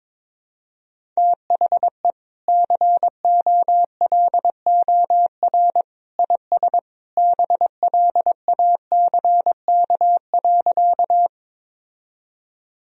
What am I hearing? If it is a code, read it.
Morse code, heard as a keyed tone, about 22 wpm, 700 Hz. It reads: THE COLOR IS BLACK.
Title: Sentences from Top 1000 Words - Review Part 2 - 22wpm